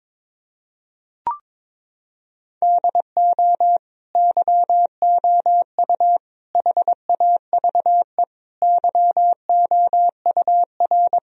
Morse code, heard as a keyed tone, about 22 wpm, 700 Hz. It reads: DO YOU HAVE YOUR
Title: Sentences from Top 1000 Words - Review Part 3 - 22wpm